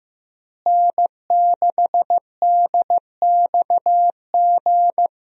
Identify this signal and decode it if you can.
Morse code, heard as a keyed tone, about 15 wpm, 700 Hz. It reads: N6DXG
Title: USA Call Signs - 5 Characters - Repeated 3x - 15wpm